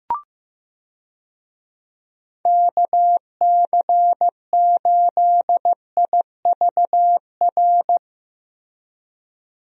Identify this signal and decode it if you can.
Morse code, heard as a keyed tone, about 15 wpm, 700 Hz. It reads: KC8IVR
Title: USA Call Signs - 6 Characters - Repeated 2x - 15wpm